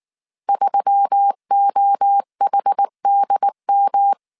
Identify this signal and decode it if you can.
Morse code, heard as a keyed tone, about 19 wpm, 800 Hz. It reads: 3OHDM
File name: CW / DET / 1 6 1500C S A 18wpm 2255